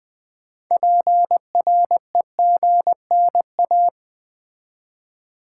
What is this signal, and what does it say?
Morse code, heard as a keyed tone, about 20 wpm, 700 Hz. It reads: PREGNA
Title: Build A Word - 3 Letters At A Time - 9-Letter Words - 20wpm